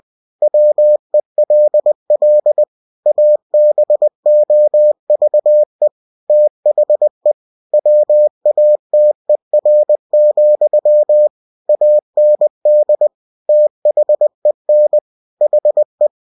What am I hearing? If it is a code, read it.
Morse code, heard as a keyed tone, about 20 wpm, 600 Hz. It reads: WELL ABOVE THE WATER, AND THEN HE
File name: cw-books / winnie-the-pooh / winnie-20wpm-page-0094